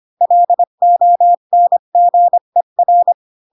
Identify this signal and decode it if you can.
Morse code, heard as a keyed tone, about 25 wpm, 700 Hz. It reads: LONGER
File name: Pinocchio / Pinocchio ch3 20 25 ik1hge